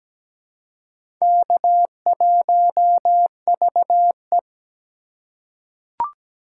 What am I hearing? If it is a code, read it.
Morse code, heard as a keyed tone, about 17 wpm, 700 Hz. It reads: K1VE
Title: USA Call Signs - 4x Character Spacing - 17wpm